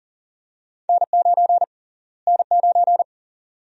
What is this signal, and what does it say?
Morse code, heard as a keyed tone, about 40 wpm, 700 Hz. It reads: N9 N9
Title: USA Call Sign Prefixes - Repeated 2x - 40wpm